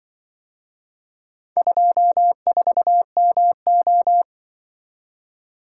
Morse code, heard as a keyed tone, about 24 wpm, 700 Hz. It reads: 24MO
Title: Speed-Racing - Four Letter-Number Combinations - 20wpm